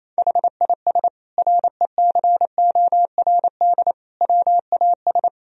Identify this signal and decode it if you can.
Morse code, heard as a keyed tone, about 27 wpm, 700 Hz. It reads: HIS RECORD WAS
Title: Sentences from Top 1000 Words - Review Part 3 - 28wpm